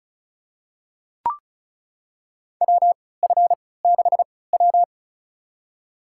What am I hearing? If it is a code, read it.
Morse code, heard as a keyed tone, about 36 wpm, spaced as for 20 wpm, 700 Hz. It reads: WF6W
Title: USA Call Signs - 4 Characters - 3x Character Spacing - 35wpm